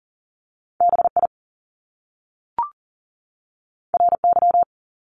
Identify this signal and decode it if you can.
Morse code, heard as a keyed tone, about 38 wpm, 700 Hz. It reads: BI RY
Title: Top Two Letter Combinations in Words 40wpm